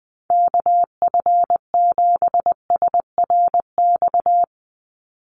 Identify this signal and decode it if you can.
Morse code, heard as a keyed tone, about 20 wpm, 700 Hz. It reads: KF7SRX
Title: USA Call Signs 20wpm